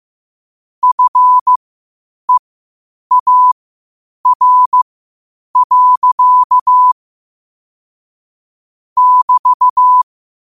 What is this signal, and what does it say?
Morse code, heard as a keyed tone, about 15 wpm, spaced as for 8 wpm, 1000 Hz. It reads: FEAR. <BT>
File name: qotdfiles / 8wpm